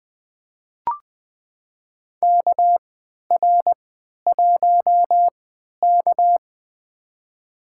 Morse code, heard as a keyed tone, about 20 wpm, spaced as for 11 wpm, 700 Hz. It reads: KR1K
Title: USA Call Signs - 4 Characters - 3x Character Spacing - 20wpm